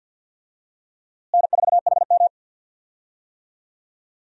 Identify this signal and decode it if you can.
Morse code, heard as a keyed tone, about 46 wpm, 700 Hz. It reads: N4HM
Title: USA Call Signs - 4 Characters - Repeated 3x - 50wpm